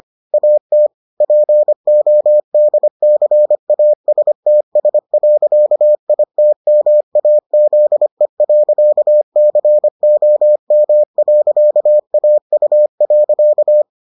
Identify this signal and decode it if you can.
Morse code, heard as a keyed tone, about 25 wpm, 600 Hz. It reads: AT PODCASTS.ITMAZE.COM.AU.